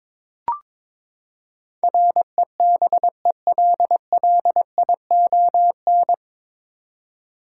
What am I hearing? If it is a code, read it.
Morse code, heard as a keyed tone, about 22 wpm, 700 Hz. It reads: REBELLION
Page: Morse Code Ninja practice 22wpm